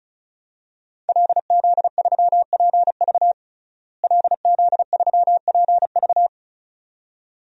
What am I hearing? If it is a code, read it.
Morse code, heard as a keyed tone, about 36 wpm, 700 Hz. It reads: LZ3PV LZ3PV
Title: DXCC-100 Call Signs - 5 Characters - Repeated 2x - 35wpm